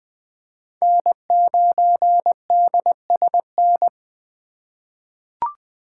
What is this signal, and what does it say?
Morse code, heard as a keyed tone, about 20 wpm, 700 Hz. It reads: N9DSN
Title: USA Call Signs - 5 Characters - 2x Character Spacing - 20wpm